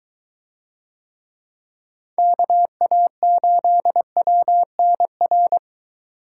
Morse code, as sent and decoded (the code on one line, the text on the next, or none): -.- .- ---.. .-- -. .-.
KA8WNR